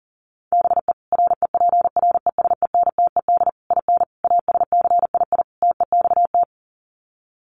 Morse code, heard as a keyed tone, about 40 wpm, 700 Hz. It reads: BE REPRESENTED IN ASCII TEXT
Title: Sets of 5 Words 40wpm